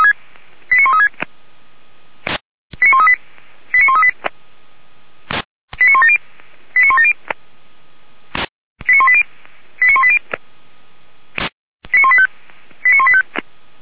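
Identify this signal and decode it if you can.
Five-tone selective call (ZVEI), heard as quick runs of five tones, tones of about 70 ms. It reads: 89136 89138 89138 89170 89170 89180 89180 89185 89185
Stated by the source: ZVEI five-tone selective call 89135 received with an SDR, from the feuerwehr-orlamuende (Orlamünde fire brigade) server.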